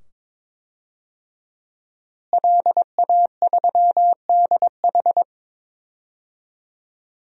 Morse code, heard as a keyed tone, about 21 wpm, 700 Hz. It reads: LA3DH
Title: DXCC-100 Call Signs - 5 Characters - Repeated 2x - 22wpm